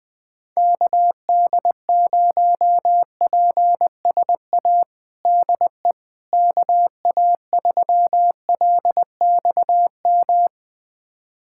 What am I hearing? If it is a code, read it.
Morse code, heard as a keyed tone, about 20 wpm, 700 Hz. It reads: KD0PSA DE KA3LXM
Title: USA Callsign DE Callsign - 6 Characters - 20wpm